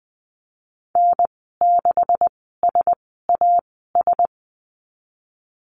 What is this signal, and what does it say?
Morse code, heard as a keyed tone, about 20 wpm, spaced as for 15 wpm, 700 Hz. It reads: N6SAS
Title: 154 Lesson 25g - Letter p - Call Signs 20wpm